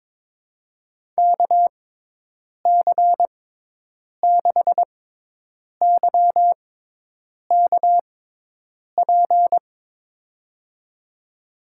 Morse code, heard as a keyed tone, about 22 wpm, spaced as for 8 wpm, 700 Hz. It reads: KC6YKP
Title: USA Call Signs - 6 Characters - 6x Character Spacing - 22wpm